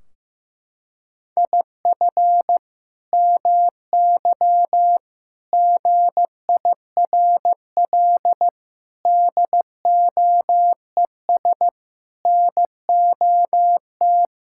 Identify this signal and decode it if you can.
Morse code, heard as a keyed tone, about 16 wpm, 700 Hz. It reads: IF MY GIRL DOES NOT
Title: Sentences from Top 1600 Words 15wpm